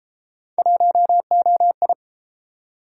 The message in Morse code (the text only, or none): 1OI